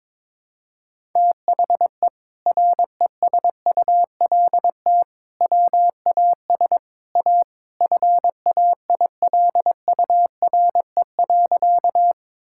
Morse code, heard as a keyed tone, about 22 wpm, 700 Hz. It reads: THE RESULT WAS A FAILURE.
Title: Sentences from Top 1000 Words - Review Part 6 - 22wpm